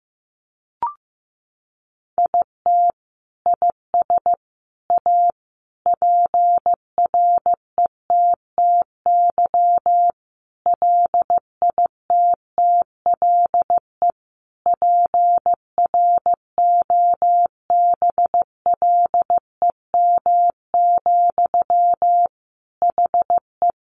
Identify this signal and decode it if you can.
Morse code, heard as a keyed tone, about 15 wpm, 700 Hz. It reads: IT IS A PRETTY LITTLE PROBLEM, HE
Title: Sentences from Top 1000 Words 15wpm